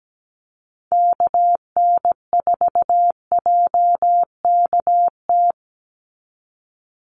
Morse code, heard as a keyed tone, about 17 wpm, 700 Hz. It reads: KN4JKT
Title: USA Call Signs 17wpm